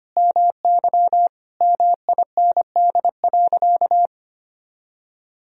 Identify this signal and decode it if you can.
Morse code, heard as a keyed tone, about 25 wpm, 700 Hz. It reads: MY MIND.
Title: Sentences from Top 1000 Words - Review Part 1 - 25wpm